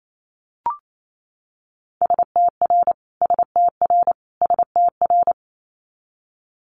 Morse code, heard as a keyed tone, about 28 wpm, 700 Hz. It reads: STR STR STR
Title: Multi-letter Phonemes Repeated 3x 28wpm